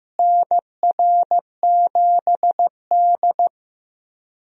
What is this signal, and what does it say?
Morse code, heard as a keyed tone, about 15 wpm, 700 Hz. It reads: NR7D
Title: USA Call Signs - 4 Characters - Repeated 3x - 15wpm